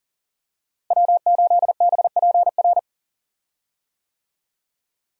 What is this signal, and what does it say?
Morse code, heard as a keyed tone, about 40 wpm, 700 Hz. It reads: W8BPR
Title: USA Call Signs - 5 Characters - Repeated 3x - 40wpm